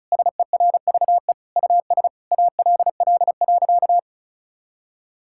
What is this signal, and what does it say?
Morse code, heard as a keyed tone, about 36 wpm, 700 Hz. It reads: SERVE US ALL.
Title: Sentences from Top 1000 Words - Review Part 5 - 35wpm